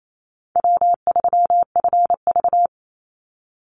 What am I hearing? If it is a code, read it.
Morse code, heard as a keyed tone, about 28 wpm, 700 Hz. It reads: W3FV
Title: Easy Call Signs - 28wpm